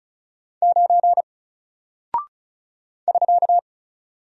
Morse code, heard as a keyed tone, about 36 wpm, 700 Hz. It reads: 9 <SK>
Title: Single Character (Letter:Number:Punctuation:Prosign) Alt 35wpm